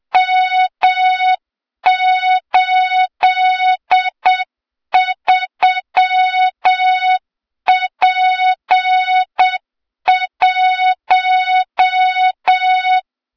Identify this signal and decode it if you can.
Morse code, heard as a keyed tone, about 7 wpm, 3700 Hz, with noise. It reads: M83P1
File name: Morze-HA2MN / hang / 23 GYAKORLAT